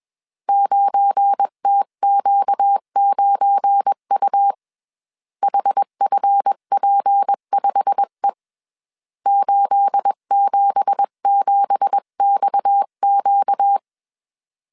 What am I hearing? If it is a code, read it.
Morse code, heard as a keyed tone, about 21 wpm, 800 Hz. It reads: 9TQ9U HFP5E 877XQ
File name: CW / DET / 1 9 1500C S B 20wpm 2313